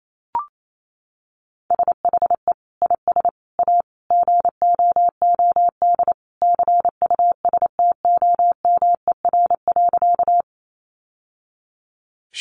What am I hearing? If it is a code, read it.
Morse code, heard as a keyed tone, about 28 wpm, 700 Hz. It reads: SHE IS A GOOD CUSTOMER.
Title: Sentences from Top 1100 Words 28wpm